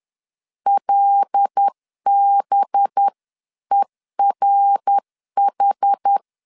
Morse code, heard as a keyed tone, about 10 wpm, 800 Hz. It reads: LBERH
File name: CW / DET / 1 6 1500C S C 10wpm 4252